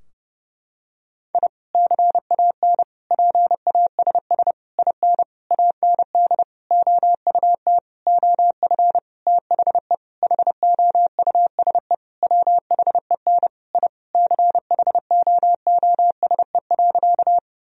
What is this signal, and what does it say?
Morse code, heard as a keyed tone, about 32 wpm, 700 Hz. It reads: I CAN PASS IN AND OUT OF THE HOUSE WHEN I CHOOSE.
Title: Sentences from Top 1000 Words - Review Part 4 - 30wpm